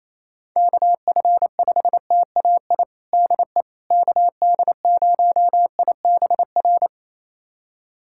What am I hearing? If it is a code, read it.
Morse code, heard as a keyed tone, about 28 wpm, 700 Hz. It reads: KF5TAI DE KD0IBR
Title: USA Callsign DE Callsign 28wpm